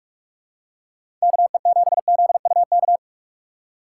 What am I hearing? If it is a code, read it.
Morse code, heard as a keyed tone, about 46 wpm, 700 Hz. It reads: KE7ZUK
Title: USA Call Signs - 6x Character Spacing - 45wpm